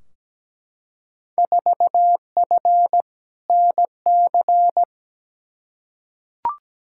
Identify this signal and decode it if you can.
Morse code, heard as a keyed tone, about 17 wpm, 700 Hz. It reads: 4F NC
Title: Field Day Exchanges - 17wpm